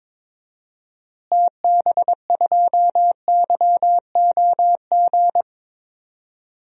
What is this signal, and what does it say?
Morse code, heard as a keyed tone, about 22 wpm, 700 Hz. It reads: TB2YOG
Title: DXCC-100 Call Signs - 6 Characters - Repeated 2x - 22wpm